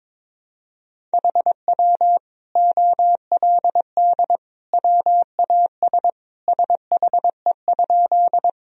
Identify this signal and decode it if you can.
Morse code, heard as a keyed tone, about 22 wpm, 700 Hz. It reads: HW OLD WAS SHE?
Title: Sentences from Top 200 Words and Abbreviations 22wpm